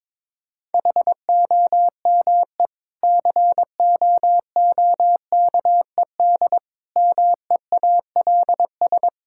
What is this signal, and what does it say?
Morse code, heard as a keyed tone, about 22 wpm, 700 Hz. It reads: HOME COOKED MEALS
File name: Difficult Sentences with Hints - 22wpm